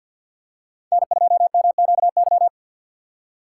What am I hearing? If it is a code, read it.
Morse code, heard as a keyed tone, about 53 wpm, 700 Hz. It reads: N2MQY
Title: USA Call Signs - 6x Character Spacing - 50wpm